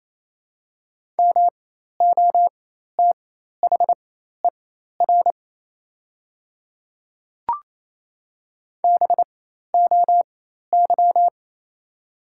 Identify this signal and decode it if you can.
Morse code, heard as a keyed tone, about 28 wpm, spaced as for 13 wpm, 700 Hz. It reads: MOTHER BOY
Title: 4x Character Spacing - Top 500 Words - 28wpm